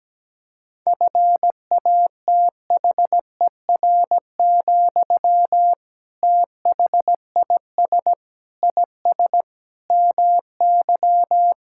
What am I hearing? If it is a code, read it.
Morse code, heard as a keyed tone, about 17 wpm, 700 Hz. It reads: FATHER, THIS IS MY
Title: Sentences from Top 1000 Words - Review Part 6 - 17wpm